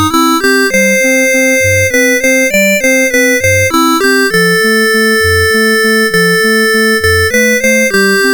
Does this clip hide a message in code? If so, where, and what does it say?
No.